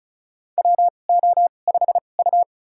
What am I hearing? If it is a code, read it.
Morse code, heard as a keyed tone, about 36 wpm, spaced as for 25 wpm, 700 Hz. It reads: WO5U
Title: USA Call Signs - 4 Characters - 2x Character Spacing - 35wpm